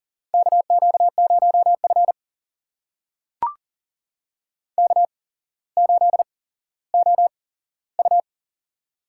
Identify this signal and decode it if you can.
Morse code, heard as a keyed tone, about 41 wpm, spaced as for 17 wpm, 700 Hz. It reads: KQ0F K8OU
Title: USA Call Signs - 4 Characters - 8x Character Spacing - 40wpm